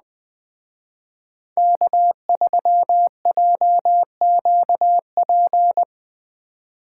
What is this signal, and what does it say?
Morse code, heard as a keyed tone, about 21 wpm, 700 Hz. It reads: K3JQP